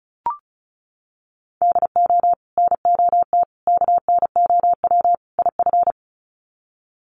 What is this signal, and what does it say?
Morse code, heard as a keyed tone, about 34 wpm, 700 Hz. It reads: DO NOT KNOW IF
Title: Sets of 4 Words 35wpm